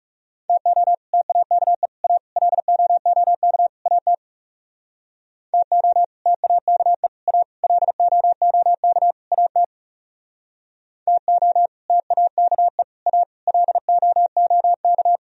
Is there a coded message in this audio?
Yes: Morse code from 0.5 to 15.2 s, heard as a keyed tone, about 38 wpm, 700 Hz. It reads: TO TAKE A LOOK AT TO TAKE A LOOK AT TO TAKE A LOOK